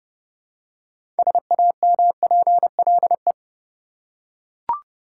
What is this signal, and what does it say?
Morse code, heard as a keyed tone, about 29 wpm, 700 Hz. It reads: SAMPLE
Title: Six Letter Words - 30wpm